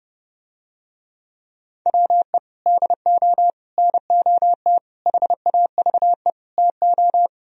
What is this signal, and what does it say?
Morse code, heard as a keyed tone, about 30 wpm, 700 Hz. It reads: WE DO NOT HAVE TO